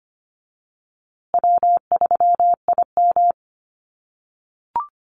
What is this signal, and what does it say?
Morse code, heard as a keyed tone, about 25 wpm, 700 Hz. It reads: W3IM
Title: USA Call Signs Repeated 2x 25wpm